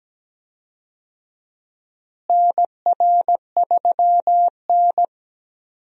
Morse code, heard as a keyed tone, about 17 wpm, 700 Hz. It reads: NR3N
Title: USA Call Signs - 4 Characters - Repeated 2x - 17wpm